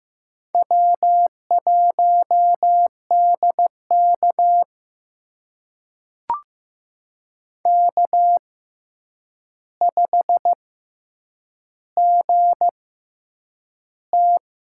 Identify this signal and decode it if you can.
Morse code, heard as a keyed tone, about 15 wpm, spaced as for 8 wpm, 700 Hz. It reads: W1DK K5GT
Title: USA Call Signs - 6x Character Spacing - 15wpm